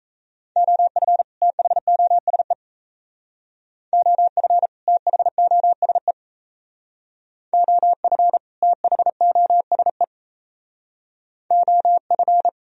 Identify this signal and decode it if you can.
Morse code, heard as a keyed tone, about 34 wpm, 700 Hz. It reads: OF THOSE OF THOSE OF THOSE OF
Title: Speed-Racing - Sets of 2 Words 28wpm